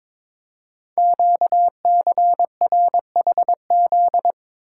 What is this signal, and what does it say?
Morse code, heard as a keyed tone, about 22 wpm, 700 Hz. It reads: QCRHZ
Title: Code Groups - 5 Letters - 22wpm